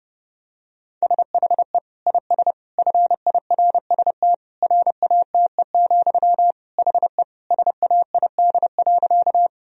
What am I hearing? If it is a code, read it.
Morse code, heard as a keyed tone, about 30 wpm, 700 Hz. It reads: SHE IS FIRST RATE, HE SAID.
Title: Sentences from Top 1000 Words - Review Part 4 - 30wpm